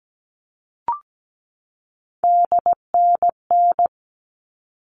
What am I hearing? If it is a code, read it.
Morse code, heard as a keyed tone, about 17 wpm, 700 Hz. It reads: DNN
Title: All Cut Numbers - 3-Digit 17wpm